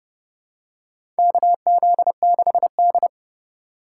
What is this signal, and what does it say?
Morse code, heard as a keyed tone, about 30 wpm, 690 Hz. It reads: KZ6D
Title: USA Call Signs - 4 Characters - 2x Character Spacing - 30wpm